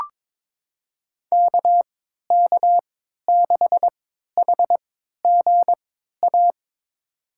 Morse code, heard as a keyed tone, about 22 wpm, spaced as for 13 wpm, 700 Hz. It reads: KK6HGA
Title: USA Call Signs - 6 Characters - 3x Character Spacing - 22wpm